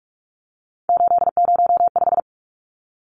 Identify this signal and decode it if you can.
Morse code, heard as a keyed tone, about 44 wpm, 700 Hz. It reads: 805
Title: Numbers - 3-Digit 45wpm